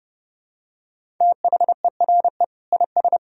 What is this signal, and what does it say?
Morse code, heard as a keyed tone, about 32 wpm, 700 Hz. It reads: THERE IS
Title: Sentences from Top 1000 Words - Review Part 1 - 30wpm